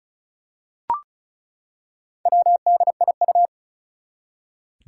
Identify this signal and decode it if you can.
Morse code, heard as a keyed tone, about 36 wpm, 690 Hz. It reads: WDIU
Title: Four Challenging Letter-Number Combinations 35wpm